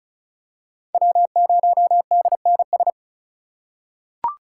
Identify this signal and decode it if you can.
Morse code, heard as a keyed tone, about 36 wpm, 690 Hz. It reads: W0DNS
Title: USA Call Signs - 5 Characters - 3x Character Spacing - 35wpm